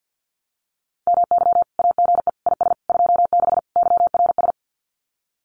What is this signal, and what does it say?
Morse code, heard as a keyed tone, about 50 wpm, 700 Hz. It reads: MY AGE IS 26 YRS
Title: Common QSO Phrases Encore 50wpm